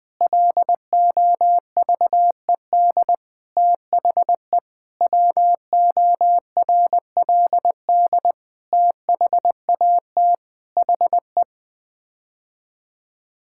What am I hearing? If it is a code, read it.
Morse code, heard as a keyed tone, about 20 wpm, 700 Hz. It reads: LOVED THE WORLD THAT HE
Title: Sets of 8 Words - 20wpm